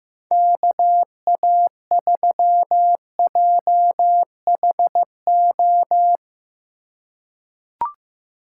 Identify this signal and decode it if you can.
Morse code, heard as a keyed tone, about 15 wpm, 700 Hz. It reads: KA3JHO
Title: USA Call Signs - 2x Character Spacing - 15wpm